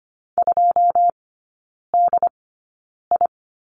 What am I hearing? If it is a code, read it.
Morse code, heard as a keyed tone, about 25 wpm, spaced as for 9 wpm, 700 Hz. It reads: 2DI
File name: course / 25:10wpm / 093 Lesson 16e - Letter c - Three Characters 25wpm